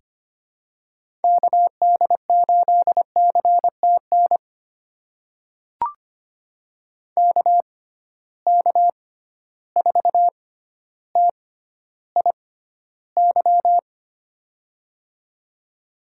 Morse code, heard as a keyed tone, about 25 wpm, spaced as for 13 wpm, 700 Hz. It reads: KD8CTN KK4TIY
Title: USA Call Signs - 6 Characters - 6x Character Spacing - 25wpm